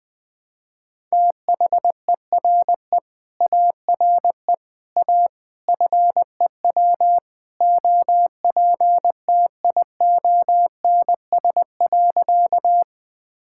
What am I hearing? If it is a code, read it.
Morse code, heard as a keyed tone, about 20 wpm, 700 Hz. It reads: THERE ARE A FEW OPTIONS.
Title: Sentences with 3-Word Phrases - 20wpm